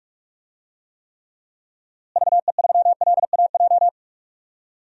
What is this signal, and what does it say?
Morse code, heard as a keyed tone, about 46 wpm, 700 Hz. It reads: VE3LAJ
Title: DXCC-100 Call Signs - 6 Characters - Repeated 2x - 45wpm